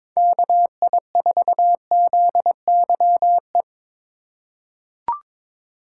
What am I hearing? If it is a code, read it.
Morse code, heard as a keyed tone, about 21 wpm, 700 Hz. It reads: KI4ZYE
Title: USA Call Signs - 3x Character Spacing - 22wpm